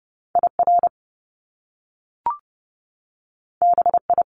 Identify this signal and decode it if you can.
Morse code, heard as a keyed tone, about 30 wpm, 700 Hz. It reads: IR BI
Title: Top Two Letter Combinations in Words 30wpm